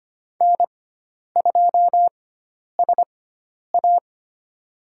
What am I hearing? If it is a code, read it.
Morse code, heard as a keyed tone, about 25 wpm, spaced as for 10 wpm, 700 Hz. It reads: N2SA
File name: Lesson 16g - Letter c - Call Signs 25wpm